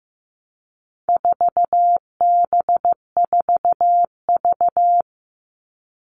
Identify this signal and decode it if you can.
Morse code, heard as a keyed tone, about 15 wpm, 700 Hz. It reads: 4B4V
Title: All Cut Numbers - 4-Digit 15wpm